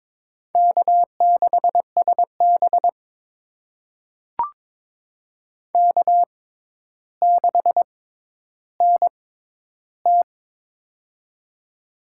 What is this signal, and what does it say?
Morse code, heard as a keyed tone, about 22 wpm, spaced as for 11 wpm, 700 Hz. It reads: K6SB K6NT
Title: USA Call Signs - 6x Character Spacing - 22wpm